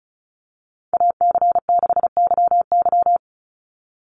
Morse code, heard as a keyed tone, about 34 wpm, 700 Hz. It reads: AC6YY
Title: Call Signs - 35wpm